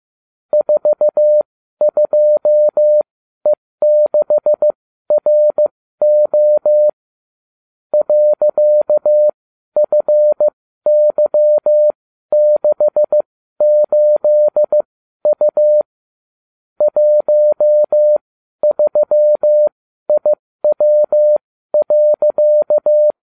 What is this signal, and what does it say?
Morse code, heard as a keyed tone, about 15 wpm, spaced as for 12 wpm, 600 Hz. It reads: 42E6RO .FY68U 13IW.